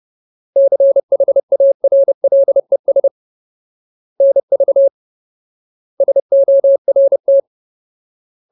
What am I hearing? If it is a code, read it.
Morse code, heard as a keyed tone, about 30 wpm, 550 Hz. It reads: CHARLES NV SORT
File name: Ultimate Word List / 30WPM Ultimate